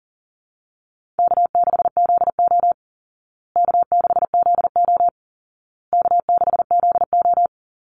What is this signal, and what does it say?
Morse code, heard as a keyed tone, about 37 wpm, 700 Hz. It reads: K6ZO K6ZO K6ZO
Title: Easy Call Signs Repeated 3x - 40wpm